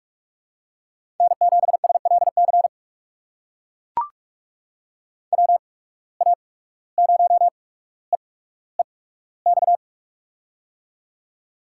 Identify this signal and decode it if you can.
Morse code, heard as a keyed tone, about 44 wpm, spaced as for 18 wpm, 700 Hz. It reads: N7SLC WA0EEX
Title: USA Call Signs - 8x Character Spacing - 45wpm